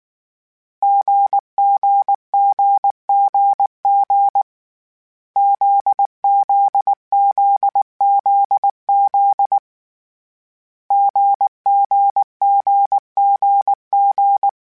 Morse code, heard as a keyed tone, about 19 wpm, 800 Hz. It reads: GGGGG ZZZZZ GGGGG